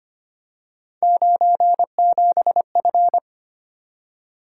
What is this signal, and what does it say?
Morse code, heard as a keyed tone, about 25 wpm, 700 Hz. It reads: 97F
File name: Three Letter-Number Combinations - 25wpm